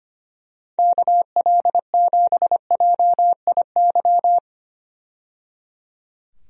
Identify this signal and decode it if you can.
Morse code, heard as a keyed tone, about 25 wpm, 700 Hz. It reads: KL7JIY
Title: USA Call Signs - 6 Characters - 25wpm